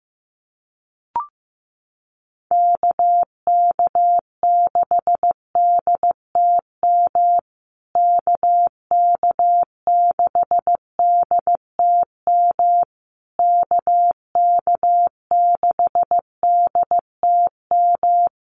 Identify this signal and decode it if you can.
Morse code, heard as a keyed tone, about 14 wpm, 700 Hz. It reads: KK6DTM KK6DTM KK6DTM
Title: USA Call Signs Repeated 3x 15wpm